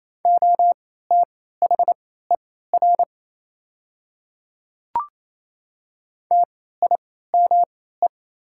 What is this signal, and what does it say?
Morse code, heard as a keyed tone, about 27 wpm, spaced as for 16 wpm, 700 Hz. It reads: OTHER TIME
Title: 3x Character Spacing - Top 200 Words - 28wpm